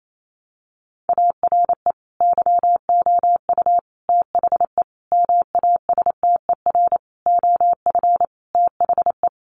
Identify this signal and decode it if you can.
Morse code, heard as a keyed tone, about 28 wpm, 700 Hz. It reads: ARE YOU THE MASTER OF THE
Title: Sentences from Top 1300 Words 28wpm